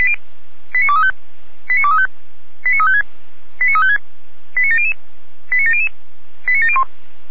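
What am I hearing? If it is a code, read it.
Five-tone selective call (ZVEI), heard as quick runs of five tones, tones of about 70 ms. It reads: 89235 89235 89356 89356 89700 89700 89701
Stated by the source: ZVEI five-tone selective call received by SDR, Feuerwehr Orlamünde.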